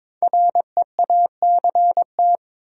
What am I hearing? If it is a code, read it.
Morse code, heard as a keyed tone, about 22 wpm, 690 Hz. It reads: REACT
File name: 5 Letter Words - First 3 Letters Given - 22wpm